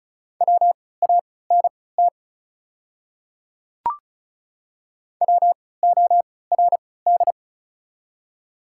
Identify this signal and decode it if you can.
Morse code, heard as a keyed tone, about 34 wpm, spaced as for 20 wpm, 700 Hz. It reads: WANT WORD
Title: 3x Character Spacing - Top 200 Words - 35wpm